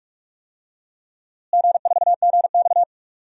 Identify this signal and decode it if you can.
Morse code, heard as a keyed tone, about 46 wpm, 700 Hz. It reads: G4GX
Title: DXCC-100 Call Signs - 4 Characters - Repeated 3x - 45wpm